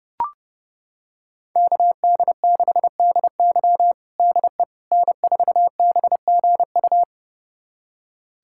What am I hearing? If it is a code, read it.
Morse code, heard as a keyed tone, about 32 wpm, 700 Hz. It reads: KD6DY DE N4BGU
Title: USA Callsign DE Callsign - 5 Characters - 30wpm